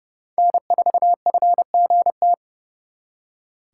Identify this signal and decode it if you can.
Morse code, heard as a keyed tone, about 37 wpm, 700 Hz. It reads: N4FGT
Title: Speed Racing - USA 1x3 Call Signs 30wpm